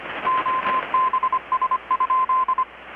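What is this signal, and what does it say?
Morse code, heard as a keyed tone, about 25 wpm, 1040 Hz, with radio noise. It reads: OBS?
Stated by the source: Morse from KPH's Night of Nights XX maritime radiotelegraphy event, received in Chiapas, Mexico about 3,800 km away on a Kenwood R-600 with a grounded 10 m long-wire antenna.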